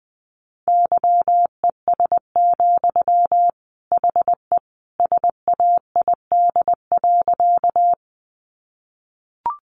Morse code, heard as a keyed tone, about 20 wpm, 700 Hz. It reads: YES, HE SAID.
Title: Sentences from Top 500 Words 20wpm